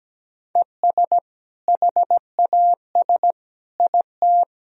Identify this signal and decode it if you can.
Morse code, heard as a keyed tone, about 17 wpm, 700 Hz. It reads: ES HAS IT
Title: Sentences from Top 100 Words and Abbreviations 17wpm